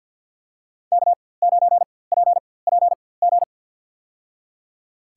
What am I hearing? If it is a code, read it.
Morse code, heard as a keyed tone, about 52 wpm, spaced as for 23 wpm, 700 Hz. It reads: K9PPG
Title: USA Call Signs - 5 Characters - 4x Character Spacing - 50wpm